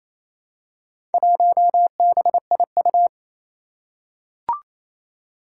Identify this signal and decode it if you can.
Morse code, heard as a keyed tone, about 27 wpm, 700 Hz. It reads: 1BIU
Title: Four Challenging Letter-Number Combinations 28wpm